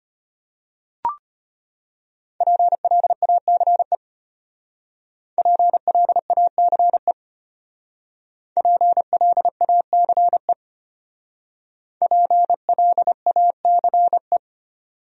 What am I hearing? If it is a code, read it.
Morse code, heard as a keyed tone, about 32 wpm, 700 Hz. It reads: PLACE PLACE PLACE PLACE